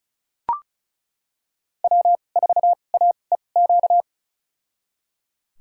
Morse code, heard as a keyed tone, about 34 wpm, spaced as for 25 wpm, 700 Hz. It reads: W4AEQ
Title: USA Call Signs - 5 Characters - 2x Character Spacing - 35wpm